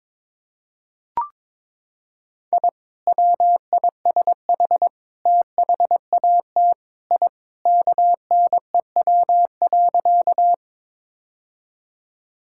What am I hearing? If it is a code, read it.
Morse code, heard as a keyed tone, about 22 wpm, 700 Hz. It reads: I WISH THAT I KNEW.
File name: practice / 22wpm / Sentences with 2-Word Phrases - 22wpm